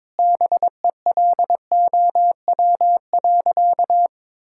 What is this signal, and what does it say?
Morse code, heard as a keyed tone, about 22 wpm, 700 Hz. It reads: BELOW.
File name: Sentences with 9-Word Phrases - 22wpm